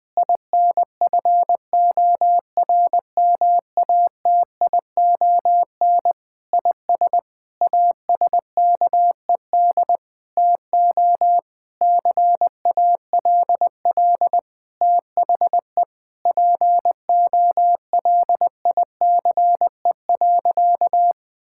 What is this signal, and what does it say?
Morse code, heard as a keyed tone, about 20 wpm, 700 Hz. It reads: INFORMATION IS ASKED TO CALL THE POLICE.
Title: Sentences with 6-Word Phrases - 20wpm